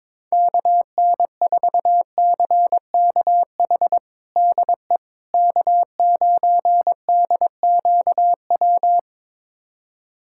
Morse code, heard as a keyed tone, about 22 wpm, 700 Hz. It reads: KN4CKH DE K9DQW
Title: USA Callsign DE Callsign 22wpm